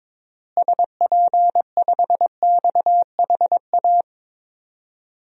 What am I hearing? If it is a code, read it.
Morse code, heard as a keyed tone, about 22 wpm, 700 Hz. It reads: SP5XHA
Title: DXCC-100 Call Signs - 6 Characters - Repeated 2x - 22wpm